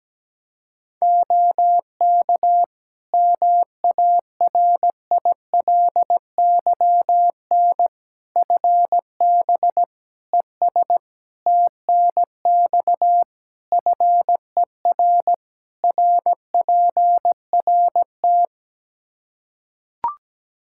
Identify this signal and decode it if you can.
Morse code, heard as a keyed tone, about 17 wpm, 700 Hz. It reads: OK MARILYN FB ES TNX FER RPRT
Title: Practice QSO - Exchange 2 - Info - 17wpm